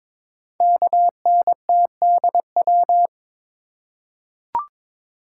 Morse code, heard as a keyed tone, about 22 wpm, 700 Hz. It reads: KNTDW
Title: Code Groups - 5 Letters - 22wpm